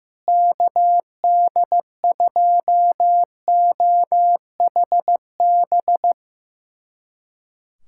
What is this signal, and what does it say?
Morse code, heard as a keyed tone, about 16 wpm, 700 Hz. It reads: KD2OHB
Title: USA Callsign DE Callsign - 6 Characters - 15wpm